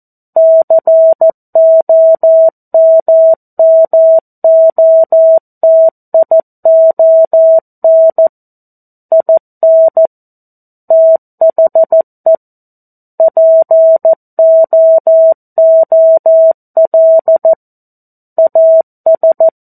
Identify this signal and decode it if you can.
Morse code, heard as a keyed tone, about 14 wpm, 650 Hz. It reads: COMMOTION IN THE POOL AS
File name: 02-Chapter-02 14 WPM